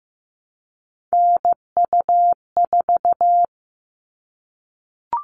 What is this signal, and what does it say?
Morse code, heard as a keyed tone, about 15 wpm, 700 Hz. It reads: NU4
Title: All Cut Numbers - 3-Digit 15wpm